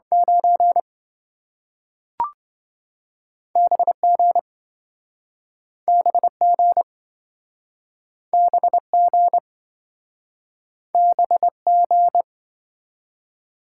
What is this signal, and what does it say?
Morse code, heard as a keyed tone, about 27 wpm, 700 Hz. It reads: 9 BG BG BG BG